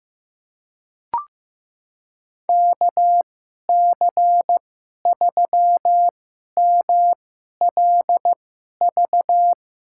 Morse code, heard as a keyed tone, about 15 wpm, spaced as for 11 wpm, 700 Hz. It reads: KC3MLV
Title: USA Call Signs - 6 Characters - 2x Character Spacing - 15wpm